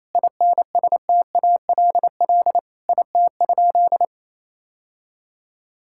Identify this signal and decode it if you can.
Morse code, heard as a keyed tone, about 28 wpm, 700 Hz. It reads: INSTALL IT?
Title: Sentences from Top 1600 Words 28wpm